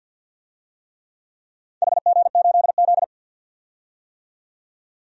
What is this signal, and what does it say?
Morse code, heard as a keyed tone, about 50 wpm, 700 Hz. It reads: HG8Z
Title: DXCC-100 Call Signs - 4 Characters - 50wpm